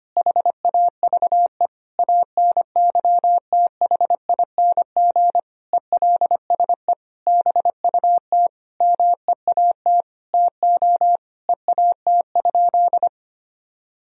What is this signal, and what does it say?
Morse code, heard as a keyed tone, about 25 wpm, 700 Hz. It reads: HAVE ANYTHING ELSE BUT MEAT TO EAT?
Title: Sentences from Top 1000 Words - Review Part 3 - 25wpm